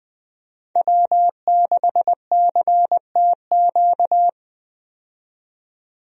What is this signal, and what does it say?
Morse code, heard as a keyed tone, about 21 wpm, 700 Hz. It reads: W6CTQ
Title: USA Call Signs - 5 Characters - 20wpm